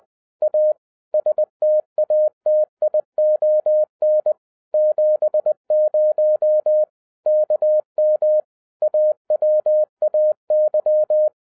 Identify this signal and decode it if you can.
Morse code, heard as a keyed tone, about 20 wpm, 610 Hz. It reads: A STATION 70 KM AWAY
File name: 2012-01-mp3 / n05-CAME BACK TO HAM RADIO AFTER 30 YEARS OF INTERVAL-20wpm